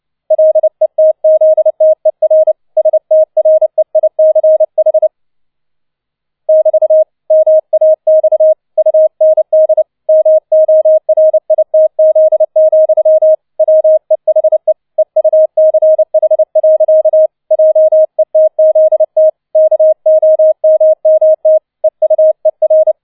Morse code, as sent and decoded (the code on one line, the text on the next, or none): .-.. - --.. - . .-. / ... - .-. .. -.-. .... / -...- / -- .- -..- / ..- -. -.. / -- --- .-. .. - --.. --..-- / .-- . .... / . ..- -.-. .... .-.-.- / .--- . - --.. - / -.- --- -- -- - / . ..- . .-.
LETZTER STREICH <BT> MAX UND MORITZ, WEHE EUCH. JETZT KOMMT EUER